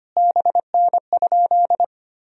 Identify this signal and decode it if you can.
Morse code, heard as a keyed tone, about 25 wpm, 700 Hz. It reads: BN?
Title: Sentences from Top 400 Words and Abbreviations 25wpm